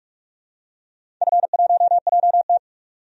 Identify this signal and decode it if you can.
Morse code, heard as a keyed tone, about 46 wpm, 700 Hz. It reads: F1JT
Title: DXCC-100 Call Signs - 4 Characters - 45wpm